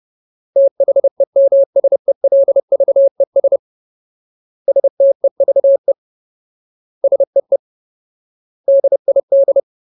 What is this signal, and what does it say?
Morse code, heard as a keyed tone, about 30 wpm, 550 Hz. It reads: THEMSELVES STEVE SEE DID
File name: Ultimate Word List / 30WPM Ultimate